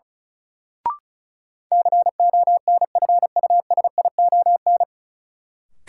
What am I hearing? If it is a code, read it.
Morse code, heard as a keyed tone, about 34 wpm, 700 Hz. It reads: CONFUSION
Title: Sentences from Top 1600 Words - New Word Drill - 35wpm